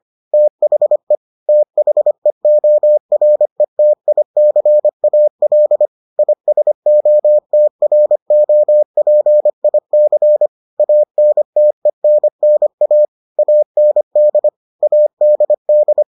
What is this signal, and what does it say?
Morse code, heard as a keyed tone, about 25 wpm, 600 Hz. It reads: THE THEORETICAL ISOTROPIC ANTENNA AND ADD